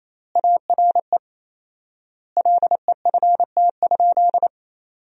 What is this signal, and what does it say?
Morse code, heard as a keyed tone, about 28 wpm, 700 Hz. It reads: ARE LEFT?
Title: 4x Word-Spacing - Sentences from Top 300 Words - 28wpm